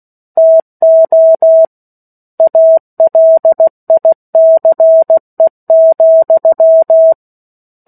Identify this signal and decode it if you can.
Morse code, heard as a keyed tone, about 16 wpm, 650 Hz. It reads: TO ALICE,
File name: 03-Chapter-03 16 WPM